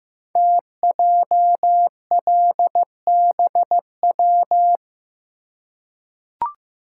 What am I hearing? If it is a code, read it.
Morse code, heard as a keyed tone, about 15 wpm, 700 Hz. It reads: TJLBW
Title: Code Groups - 5 Letter:Numbers - 15wpm